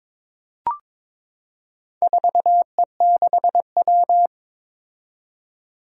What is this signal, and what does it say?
Morse code, heard as a keyed tone, about 22 wpm, 700 Hz. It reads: 4E6W
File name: practice / 22wpm / Four Letter-Number Combinations - 22wpm